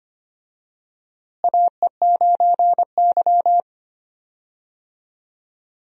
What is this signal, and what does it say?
Morse code, heard as a keyed tone, about 25 wpm, 700 Hz. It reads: AE9Y
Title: USA Call Signs - 4 Characters - Repeated 2x - 25wpm